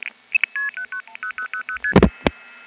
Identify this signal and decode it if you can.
Multi-frequency trunk signalling, heard as fast pairs of tones, tones of about 90 ms. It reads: 10000ST